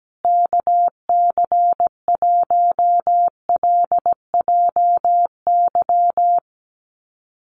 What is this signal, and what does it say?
Morse code, heard as a keyed tone, about 17 wpm, 700 Hz. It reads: KC1LJY
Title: USA Call Signs Repeated 3x 17wpm